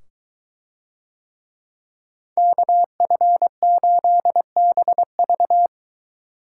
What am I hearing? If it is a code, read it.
Morse code, heard as a keyed tone, about 22 wpm, 700 Hz. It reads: KF8BV